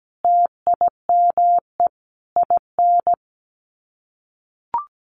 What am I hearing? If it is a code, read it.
Morse code, heard as a keyed tone, about 17 wpm, 700 Hz. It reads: TIME IN
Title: Sets of 4 Words 17wpm